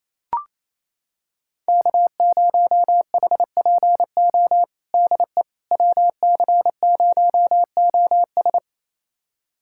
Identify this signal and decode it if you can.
Morse code, heard as a keyed tone, about 28 wpm, 700 Hz. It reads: K0HPO DE WC0OS
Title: USA Callsign DE Callsign - 5 Characters - 28wpm